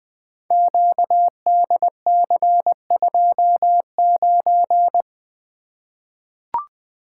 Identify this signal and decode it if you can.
Morse code, heard as a keyed tone, about 20 wpm, 700 Hz. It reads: QDC29
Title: Code Groups - 5 Letter:Numbers - 20wpm